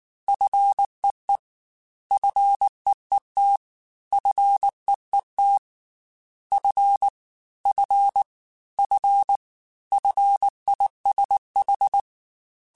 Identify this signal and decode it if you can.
Morse code, heard as a keyed tone, about 19 wpm, 800 Hz. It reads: FEE FEET FEET F F F FISH
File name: F and L / F and L mp3 / F and L Prefixes 3x2 Twice @18